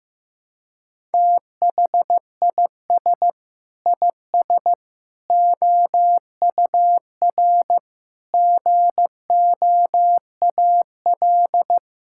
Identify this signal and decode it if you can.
Morse code, heard as a keyed tone, about 15 wpm, 700 Hz. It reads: THIS IS OUR GOAL
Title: Sentences from Top 1000 Words - Review Part 5 - 15wpm